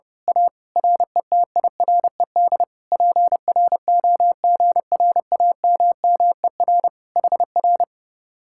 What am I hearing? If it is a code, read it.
Morse code, heard as a keyed tone, about 28 wpm, 700 Hz. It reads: A RETIRED PROGRAMMER HR
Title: Speed-Racing Common QSO Phrases - 30wpm